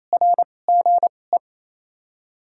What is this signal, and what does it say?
Morse code, heard as a keyed tone, about 28 wpm, spaced as for 20 wpm, 700 Hz. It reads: RGE
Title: 2x Character Spacing - Top 200 Words - 28wpm